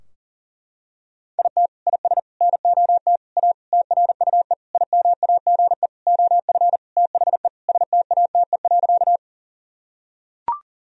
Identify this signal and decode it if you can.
Morse code, heard as a keyed tone, about 40 wpm, 700 Hz. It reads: IT IS NOT A TRUE IMAGE OF THE STATE.
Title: Sentences from Top 1000 Words - Review Part 4 - 40wpm